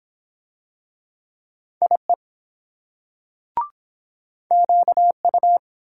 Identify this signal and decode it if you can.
Morse code, heard as a keyed tone, about 23 wpm, 700 Hz. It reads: IE QU